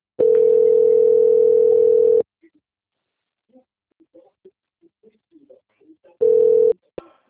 Ringback tone (North America), 440 + 480 Hz.
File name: out- - - - - .5822